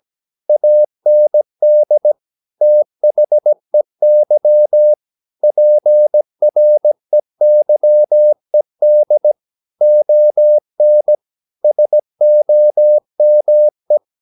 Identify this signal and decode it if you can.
Morse code, heard as a keyed tone, about 17 wpm, 600 Hz. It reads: AND THEY PREYED ON SOME